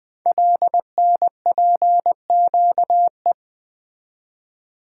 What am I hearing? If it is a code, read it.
Morse code, heard as a keyed tone, about 20 wpm, 700 Hz. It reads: LNPQE
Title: Code Groups - 5 Letters - 20wpm